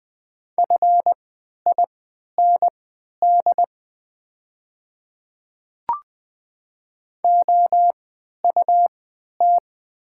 Morse code, heard as a keyed tone, about 20 wpm, spaced as for 11 wpm, 700 Hz. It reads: FIND OUT
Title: 3x Character Spacing - Top 100 Words - 20wpm